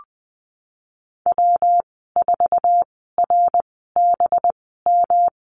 Morse code, heard as a keyed tone, about 20 wpm, spaced as for 15 wpm, 700 Hz. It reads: W4RBM
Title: 200 Lesson 32g - Letter k - Call Signs 20wpm